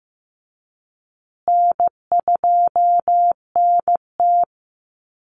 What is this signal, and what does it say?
Morse code, heard as a keyed tone, about 15 wpm, 700 Hz. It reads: N2NT
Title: Easy Call Signs Repeated 3x - 15wpm